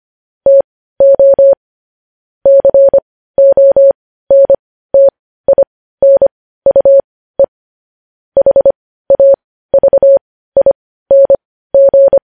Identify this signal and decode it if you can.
Morse code, heard as a keyed tone, about 25 wpm, spaced as for 15 wpm, 550 Hz. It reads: TO CONTINUE HAVING
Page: news headlines in morse code at 15 wpm